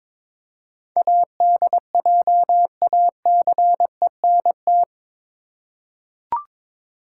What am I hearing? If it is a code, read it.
Morse code, heard as a keyed tone, about 22 wpm, 700 Hz. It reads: ADJACENT
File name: practice / 22wpm / Eight Letter Words - 22wpm